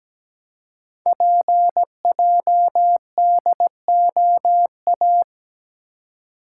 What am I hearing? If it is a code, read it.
Morse code, heard as a keyed tone, about 17 wpm, 700 Hz. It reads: PJDOA
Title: Code Groups - 5 Letters - 17wpm